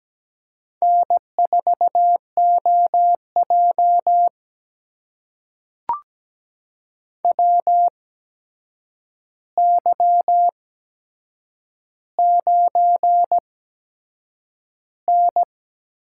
Morse code, heard as a keyed tone, about 17 wpm, spaced as for 7 wpm, 700 Hz. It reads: N4OJ WY9N
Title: USA Call Signs - 4 Characters - 8x Character Spacing - 17wpm